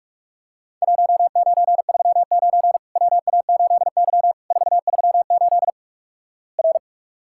Morse code, heard as a keyed tone, about 44 wpm, 700 Hz. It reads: 1939 WA8Y 438 R
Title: Worked All Europe DX Contest - QTC Exchange - 45wpm